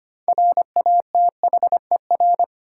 Morse code, heard as a keyed tone, about 25 wpm, 700 Hz. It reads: RATHER
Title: Sentences from Top 700 Words - New Word Drill - 25wpm